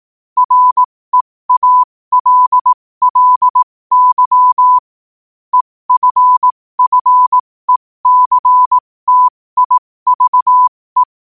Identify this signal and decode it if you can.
Morse code, heard as a keyed tone, about 18 wpm, spaced as for 15 wpm, 1000 Hz. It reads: REALLY EFFECTIVE
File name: qotdfiles / 15wpm